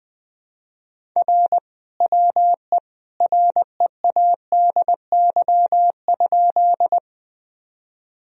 Morse code, heard as a keyed tone, about 21 wpm, 700 Hz. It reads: R WE READY?
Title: Sentences from Top 400 Words and Abbreviations 20wpm